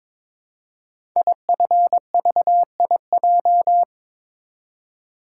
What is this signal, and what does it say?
Morse code, heard as a keyed tone, about 22 wpm, 700 Hz. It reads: IFVIJ
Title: Code Groups - 5 Letters - 22wpm